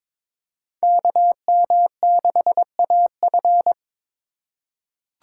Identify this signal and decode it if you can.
Morse code, heard as a keyed tone, about 22 wpm, 700 Hz. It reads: KM6AF
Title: USA Call Signs - 5 Characters - 4x Character Spacing - 22wpm